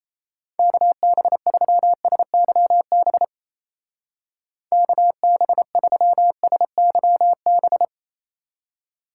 Morse code, heard as a keyed tone, about 36 wpm, 700 Hz. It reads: KB3SYB KB3SYB